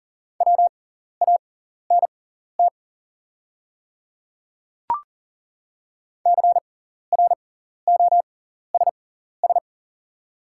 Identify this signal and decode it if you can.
Morse code, heard as a keyed tone, about 40 wpm, spaced as for 14 wpm, 690 Hz. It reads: WANT CROSS
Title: 6x Character Spacing - Top 300 Words - 40wpm